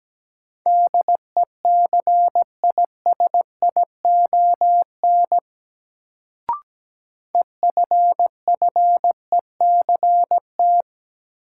Morse code, heard as a keyed tone, about 17 wpm, 700 Hz. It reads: DECISION EFFECT
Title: Sentences from Top 1000 Words - New Word Drill - 17wpm